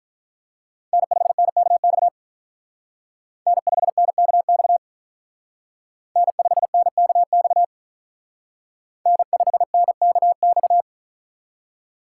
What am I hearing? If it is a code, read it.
Morse code, heard as a keyed tone, about 42 wpm, 700 Hz. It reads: N5NKX N5NKX N5NKX N5NKX
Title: Speed Racing - USA 1x3 Call Signs 35wpm